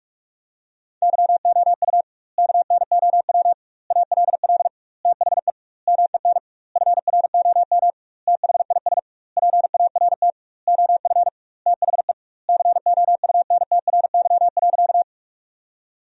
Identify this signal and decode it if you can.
Morse code, heard as a keyed tone, about 46 wpm, 700 Hz. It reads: YOU KNOW ALL THE MEN FROM THIS PART OF THE COUNTRY.
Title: 2x Word-Spacing - Sentences from Top 300 Words - 45wpm